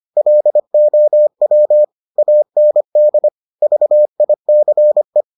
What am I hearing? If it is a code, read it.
Morse code, heard as a keyed tone, about 25 wpm, 600 Hz. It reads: LOW AND VICE